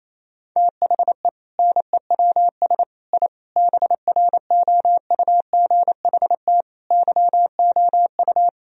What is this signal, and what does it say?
Morse code, heard as a keyed tone, about 27 wpm, 690 Hz. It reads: THE NEWS I BROUGHT YOU
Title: Sentences from Top 1000 Words - Review Part 3 - 28wpm